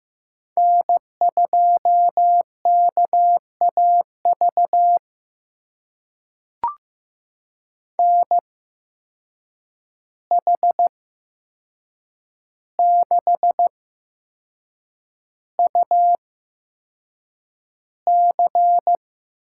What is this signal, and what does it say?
Morse code, heard as a keyed tone, about 15 wpm, spaced as for 6 wpm, 700 Hz. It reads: N2KAV NH6UC
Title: USA Call Signs - 5 Characters - 8x Character Spacing - 15wpm